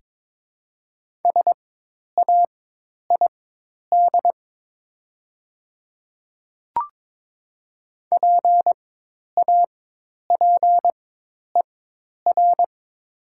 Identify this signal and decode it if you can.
Morse code, heard as a keyed tone, about 22 wpm, spaced as for 10 wpm, 700 Hz. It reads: SAID PAPER
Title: 4x Character Spacing - Top 300 Words - 22wpm